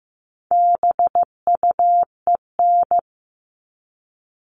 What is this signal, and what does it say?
Morse code, heard as a keyed tone, about 15 wpm, 700 Hz. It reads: BUEN
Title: All Cut Numbers - 4-Digit 15wpm